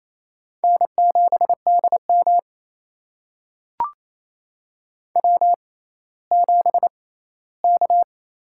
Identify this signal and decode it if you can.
Morse code, heard as a keyed tone, about 29 wpm, spaced as for 16 wpm, 700 Hz. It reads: N7DM W7K
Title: USA Call Signs - 4 Characters - 6x Character Spacing - 28wpm